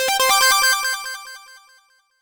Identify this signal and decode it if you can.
There is no code.